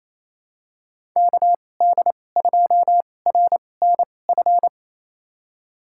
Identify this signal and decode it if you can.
Morse code, heard as a keyed tone, about 28 wpm, spaced as for 20 wpm, 700 Hz. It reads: KD2RNF
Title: USA Call Signs - 6 Characters - 2x Character Spacing - 28wpm